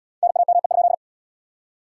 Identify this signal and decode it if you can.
Morse code, heard as a keyed tone, about 67 wpm, 700 Hz. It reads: NAME?